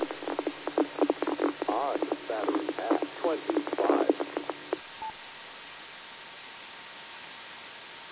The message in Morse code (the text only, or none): none